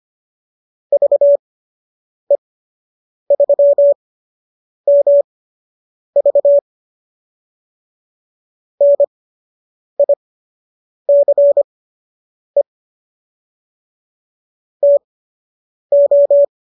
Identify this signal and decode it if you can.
Morse code, heard as a keyed tone, about 25 wpm, spaced as for 8 wpm, 590 Hz. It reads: VE3MV NICE TO